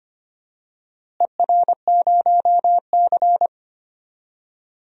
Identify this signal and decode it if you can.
Morse code, heard as a keyed tone, about 25 wpm, 700 Hz. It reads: ER0C
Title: DXCC-100 Call Signs - 4 Characters - Repeated 3x - 25wpm